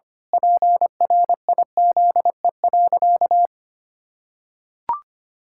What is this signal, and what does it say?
Morse code, heard as a keyed tone, about 25 wpm, 700 Hz. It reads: PRIZE.
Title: Pangram Challenge v2 - 25wpm